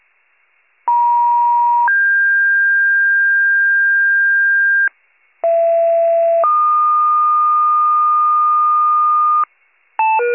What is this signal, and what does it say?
Two-tone page, heard as a one-second tone, then a three-second tone, about 950 then 1600 Hz, about 670 then 1220 Hz, with noise. It is heard from 0.9 to 9.4 s.